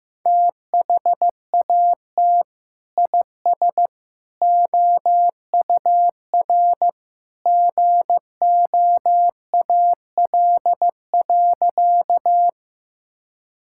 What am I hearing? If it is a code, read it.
Morse code, heard as a keyed tone, about 15 wpm, 700 Hz. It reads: THAT IS OUR GOAL.
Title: Sentences from Top 1000 Words - Review Part 5 - 15wpm